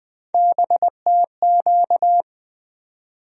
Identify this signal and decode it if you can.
Morse code, heard as a keyed tone, about 20 wpm, 700 Hz. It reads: BTQ
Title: USA Call Sign Postfixes - 3 Letters 20wpm